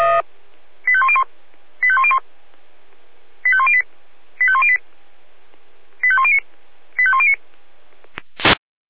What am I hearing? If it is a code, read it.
Five-tone selective call (ZVEI), heard as quick runs of five tones, tones of about 70 ms. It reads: 85101 85101 85108 85108 85109 85109
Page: /496/SDR/ZVEI/